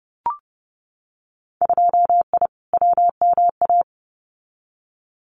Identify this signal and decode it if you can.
Morse code, heard as a keyed tone, about 30 wpm, 700 Hz. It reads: 2I WMA
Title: Winter Field Day Exchanges Encore 30wpm